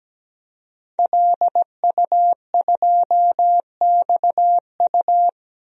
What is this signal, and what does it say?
Morse code, heard as a keyed tone, about 17 wpm, 700 Hz. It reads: LU2XU
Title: DXCC-100 Call Signs - 5 Characters - Repeated 2x - 17wpm